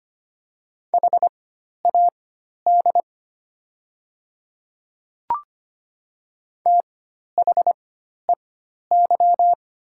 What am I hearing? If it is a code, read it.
Morse code, heard as a keyed tone, about 25 wpm, spaced as for 12 wpm, 700 Hz. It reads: HAD THEY
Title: 4x Character Spacing - Top 100 Words - 25wpm